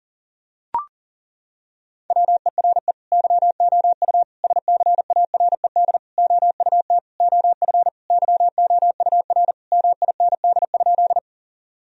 Morse code, heard as a keyed tone, about 40 wpm, 700 Hz. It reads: WERE YOU SCARED OUT OF YOUR MIND?
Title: Sentences from Top 1000 Words - Review Part 4 - 40wpm